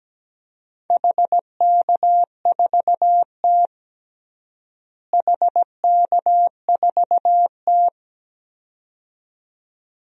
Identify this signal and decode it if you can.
Morse code, heard as a keyed tone, about 17 wpm, 700 Hz. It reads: HK4T HK4T
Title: DXCC-100 Call Signs - 4 Characters - Repeated 2x - 17wpm